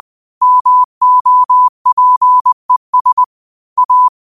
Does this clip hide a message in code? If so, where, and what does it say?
Yes: Morse code from 0.4 to 4.1 s, heard as a keyed tone, about 20 wpm, 1000 Hz. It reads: MOPES A